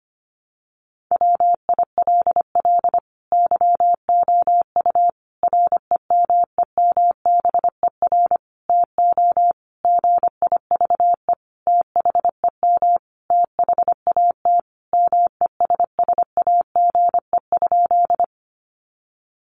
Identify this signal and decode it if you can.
Morse code, heard as a keyed tone, about 25 wpm, 700 Hz. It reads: WILL YOU REMEMBER TO GIVE THEM THAT MESSAGE?
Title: Sentences from Top 1100 Words 25wpm